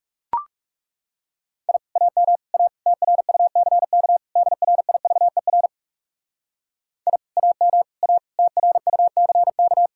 Morse code, heard as a keyed tone, about 38 wpm, 700 Hz. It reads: I AM A TRUCK DRIVER I AM A TRUCK